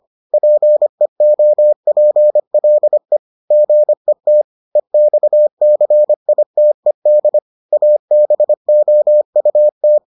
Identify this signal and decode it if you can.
Morse code, heard as a keyed tone, about 25 wpm, 600 Hz. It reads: PEOPLE GET EXCITED ABOUT